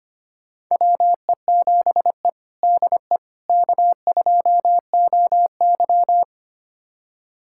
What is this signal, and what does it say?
Morse code, heard as a keyed tone, about 25 wpm, 700 Hz. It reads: WE7E DE K2OY
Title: USA Callsign DE Callsign - 4 Characters - 25wpm